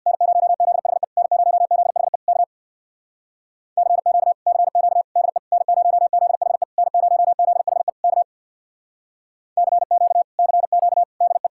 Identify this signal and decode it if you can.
Morse code, heard as a keyed tone, about 67 wpm, 700 Hz. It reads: N0ZHE N0ZHE K CQ CQ DE N0ZHE N0ZHE K CQ CQ DE